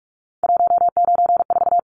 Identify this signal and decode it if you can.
Morse code, heard as a keyed tone, about 44 wpm, 700 Hz. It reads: 194
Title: Numbers - 3-Digit 45wpm